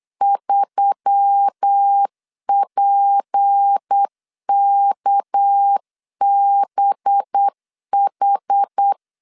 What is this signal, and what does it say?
Morse code, heard as a keyed tone, about 8 wpm, 800 Hz. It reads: 3PKBH
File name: CW / DET / 1 6 1500C S B 8wpm 5142